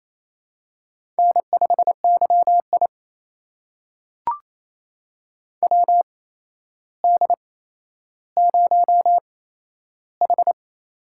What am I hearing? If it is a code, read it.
Morse code, heard as a keyed tone, about 27 wpm, spaced as for 12 wpm, 700 Hz. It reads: N5YI WD0H
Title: USA Call Signs - 4 Characters - 8x Character Spacing - 28wpm